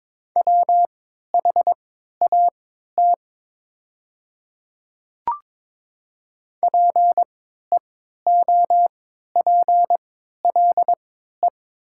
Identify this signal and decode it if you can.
Morse code, heard as a keyed tone, about 22 wpm, spaced as for 12 wpm, 700 Hz. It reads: WHAT PEOPLE
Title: 3x Character Spacing - Top 100 Words - 22wpm